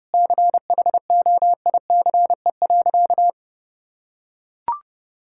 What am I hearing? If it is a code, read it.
Morse code, heard as a keyed tone, about 30 wpm, 700 Hz. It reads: CHOICE.
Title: Sentences from Top 1000 Words - Review Part 6 - 30wpm